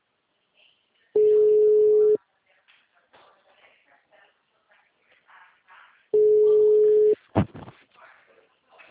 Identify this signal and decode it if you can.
Ringback tone (Europe, 425 Hz), with speech.
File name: out- -2013- - - .28576